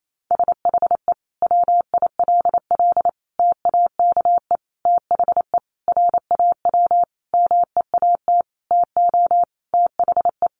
Morse code, heard as a keyed tone, about 28 wpm, 700 Hz. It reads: SHE WILL TAKE THE RAW MEAT TO THE
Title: Sentences from Top 1500 Words 28wpm